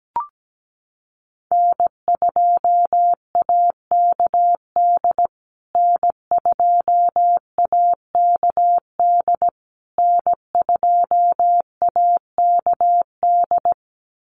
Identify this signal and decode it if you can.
Morse code, heard as a keyed tone, about 17 wpm, 700 Hz. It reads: N2AKD N2AKD N2AKD
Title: USA Call Signs Repeated 3x 17wpm